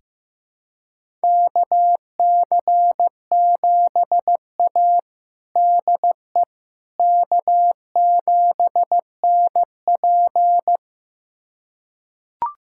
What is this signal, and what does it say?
Morse code, heard as a keyed tone, about 15 wpm, 700 Hz. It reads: KC7A DE K7NP
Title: USA Callsign DE Callsign - 4 Characters - 15wpm